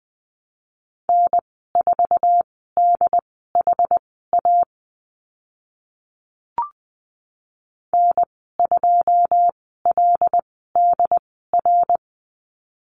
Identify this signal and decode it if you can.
Morse code, heard as a keyed tone, about 20 wpm, spaced as for 15 wpm, 700 Hz. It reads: N4DHA N2LDR
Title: 088 Lesson 15g - Number 5 - Call Signs 20wpm